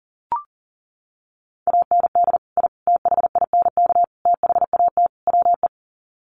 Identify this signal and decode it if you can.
Morse code, heard as a keyed tone, about 37 wpm, 700 Hz. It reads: AND I THINK THAT WE
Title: Sets of 5 Words Encore2 40wpm